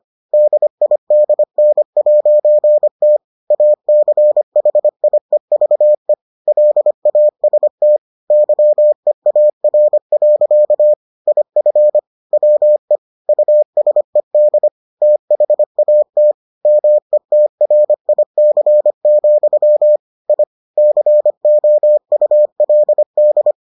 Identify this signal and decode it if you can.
Morse code, heard as a keyed tone, about 25 wpm, 600 Hz. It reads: DIDN'T ACHIEVE LAST YEAR. IF WE USED THAT METRIC, I COULD